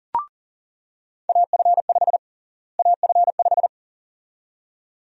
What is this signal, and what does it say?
Morse code, heard as a keyed tone, about 40 wpm, 700 Hz. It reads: AF5 AF5
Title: USA Call Sign Prefixes - Repeated 2x - 40wpm